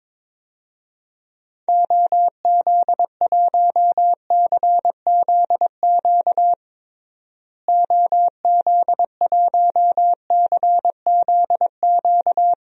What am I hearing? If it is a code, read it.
Morse code, heard as a keyed tone, about 22 wpm, 700 Hz. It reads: OZ1CZQ OZ1CZQ
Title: DXCC-100 Call Signs - 6 Characters - Repeated 2x - 22wpm